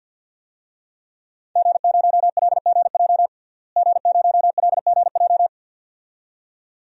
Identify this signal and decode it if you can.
Morse code, heard as a keyed tone, about 51 wpm, 700 Hz. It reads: G0LGJ G0LGJ
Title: DXCC-100 Call Signs - 5 Characters - Repeated 2x - 50wpm